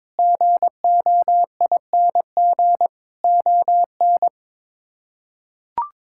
Morse code, heard as a keyed tone, about 22 wpm, 700 Hz. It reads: GOING ON
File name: practice / 22wpm / Sets of 7 Words - 22wpm